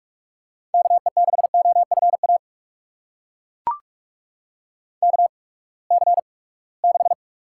Morse code, heard as a keyed tone, about 42 wpm, spaced as for 25 wpm, 700 Hz. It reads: KE6OFA KC6
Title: USA Call Signs - 6 Characters - 8x Character Spacing - 45wpm